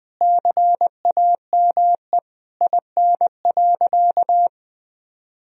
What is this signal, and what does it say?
Morse code, heard as a keyed tone, about 20 wpm, 700 Hz. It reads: CAME IN.
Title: Sentences from Top 1000 Words - Review Part 3 - 20wpm